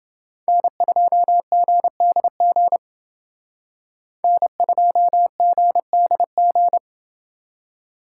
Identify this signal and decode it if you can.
Morse code, heard as a keyed tone, about 28 wpm, 700 Hz. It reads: N2GDG N2GDG